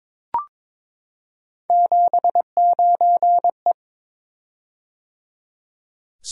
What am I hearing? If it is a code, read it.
Morse code, heard as a keyed tone, about 23 wpm, 700 Hz. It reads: 79E
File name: practice / 22wpm / Three Letter-Number Combinations - 22wpm